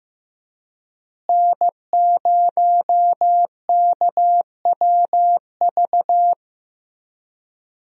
Morse code, heard as a keyed tone, about 15 wpm, 700 Hz. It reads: N0KWV
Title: USA Call Signs - 5 Characters - Repeated 3x - 15wpm